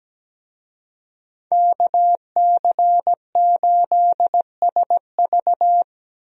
Morse code, heard as a keyed tone, about 17 wpm, 700 Hz. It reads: KC8SV
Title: USA Call Signs - 5 Characters - Repeated 3x - 17wpm